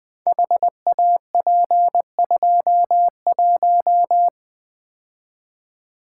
Morse code, heard as a keyed tone, about 20 wpm, 700 Hz. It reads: HAP21
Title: Code Groups - 5 Letter:Numbers - 20wpm